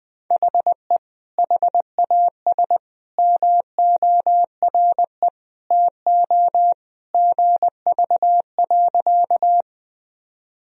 Morse code, heard as a keyed tone, about 20 wpm, 700 Hz. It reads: HE HAS MORE TO GV.
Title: Sentences from Top 200 Words and Abbreviations 20wpm